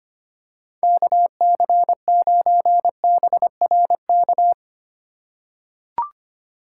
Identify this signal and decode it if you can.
Morse code, heard as a keyed tone, about 25 wpm, 700 Hz. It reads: KC9BRK
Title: USA Call Signs - 6 Characters - 3x Character Spacing - 25wpm